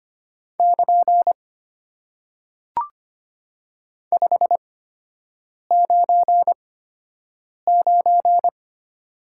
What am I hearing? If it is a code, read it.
Morse code, heard as a keyed tone, about 25 wpm, spaced as for 7 wpm, 700 Hz. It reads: <KN> 599
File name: Top 90 QSO Elements - 8x Character Spacing - 25wpm